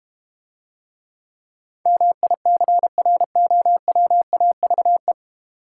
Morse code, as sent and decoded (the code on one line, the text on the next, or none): -- .. -.-. .-. --- .-- .- ...- .
MICROWAVE